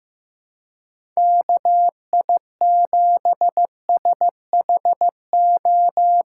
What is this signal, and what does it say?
Morse code, heard as a keyed tone, about 15 wpm, 700 Hz. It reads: KI7SHO
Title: USA Call Signs - 6 Characters - 3x Character Spacing - 15wpm